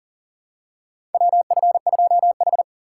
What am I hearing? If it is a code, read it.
Morse code, heard as a keyed tone, about 40 wpm, 700 Hz. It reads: WF2H
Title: USA Call Signs - 4 Characters - 2x Character Spacing - 40wpm